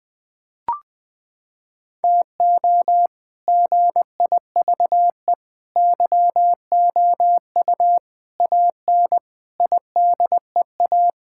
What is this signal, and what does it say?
Morse code, heard as a keyed tone, about 21 wpm, 700 Hz. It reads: TO GIVE YOU AN IDEA
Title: Sets of 6 Words - 20wpm